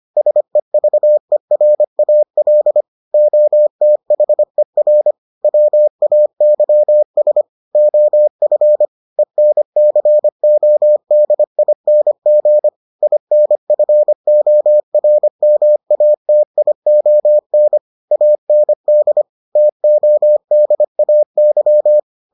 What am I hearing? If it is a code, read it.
Morse code, heard as a keyed tone, about 25 wpm, 600 Hz. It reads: SEVERAL OTHER WAYS OF ENCODING INFORMATION AND TODAY